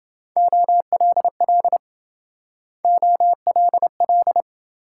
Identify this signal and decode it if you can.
Morse code, heard as a keyed tone, about 26 wpm, 700 Hz. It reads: OLL OLL